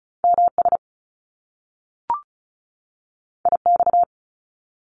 Morse code, heard as a keyed tone, about 36 wpm, 700 Hz. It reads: MS IX
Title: Top Two Letter Combinations in Words 35wpm